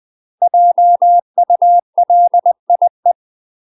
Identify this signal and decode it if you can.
Morse code, heard as a keyed tone, about 20 wpm, 700 Hz. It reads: JULIE
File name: UK MASTER QSOs TRAINING / 20WPM / UK MASTER QSOs 0015